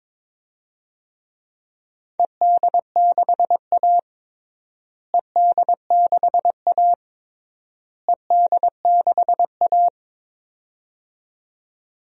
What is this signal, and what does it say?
Morse code, heard as a keyed tone, about 21 wpm, 700 Hz. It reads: ED6A ED6A ED6A
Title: DXCC-100 Call Signs - 4 Characters - Repeated 3x - 22wpm